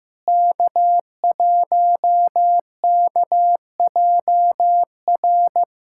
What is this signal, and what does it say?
Morse code, heard as a keyed tone, about 15 wpm, 700 Hz. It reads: K1KJR
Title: USA Callsign DE Callsign 15wpm